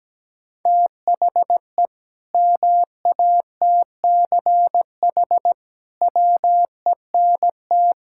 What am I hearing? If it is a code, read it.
Morse code, heard as a keyed tone, about 17 wpm, 700 Hz. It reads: THE MATCH WENT
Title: Sentences from Top 1600 Words 17wpm